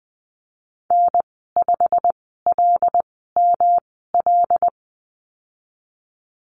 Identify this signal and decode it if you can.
Morse code, heard as a keyed tone, about 20 wpm, spaced as for 15 wpm, 700 Hz. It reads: N5LML
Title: 147 Lesson 24g - Letter y - Call Signs 20wpm